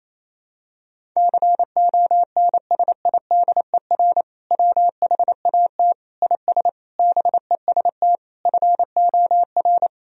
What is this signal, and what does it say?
Morse code, heard as a keyed tone, about 27 wpm, 700 Hz. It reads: CONSIDER WHAT IS BEST FOR
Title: Sentences from Top 1000 Words - Review Part 4 - 28wpm